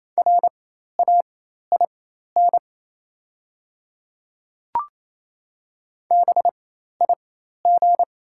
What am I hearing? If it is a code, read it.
Morse code, heard as a keyed tone, about 28 wpm, spaced as for 13 wpm, 700 Hz. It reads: RAIN BIG
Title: 4x Character Spacing - Top 500 Words - 28wpm